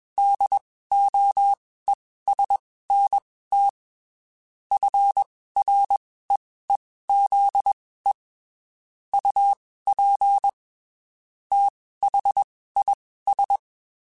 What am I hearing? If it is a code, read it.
Morse code, heard as a keyed tone, about 21 wpm, spaced as for 15 wpm, 800 Hz. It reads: DOESNT FREEZE UP THIS
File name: (spaced) / QSOs Spaced-mp3s / WD0FBT @20 sp